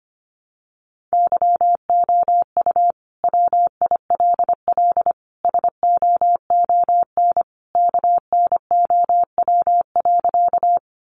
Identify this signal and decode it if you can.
Morse code, heard as a keyed tone, about 25 wpm, 700 Hz. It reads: YOU WILL SOON KNOW.
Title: Sentences from Top 400 Words 25wpm